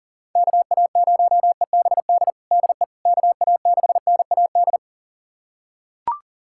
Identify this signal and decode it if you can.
Morse code, heard as a keyed tone, about 40 wpm, 700 Hz. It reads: KA0EBD DE KA6NAD
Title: USA Callsign DE Callsign - 6 Characters - 40wpm